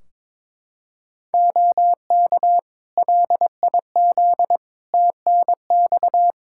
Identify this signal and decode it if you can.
Morse code, heard as a keyed tone, about 22 wpm, 700 Hz. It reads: OK LIZ TNX
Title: K1USN SST QSO Practice 22wpm